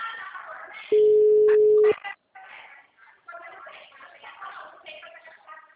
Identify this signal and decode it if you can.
Ringback tone (Europe, 425 Hz).